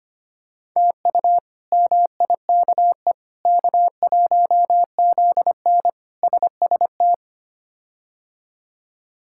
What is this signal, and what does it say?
Morse code, heard as a keyed tone, about 25 wpm, 700 Hz. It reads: TU MIKE K1ZN SST
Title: K1USN SST QSO Practice - 25wpm